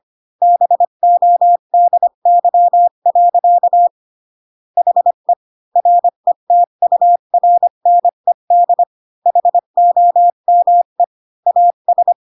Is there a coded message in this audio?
Yes: Morse code from 0.4 to 12.1 s, heard as a keyed tone, about 25 wpm, 700 Hz. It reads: BODY. HE RETURNED HOME AS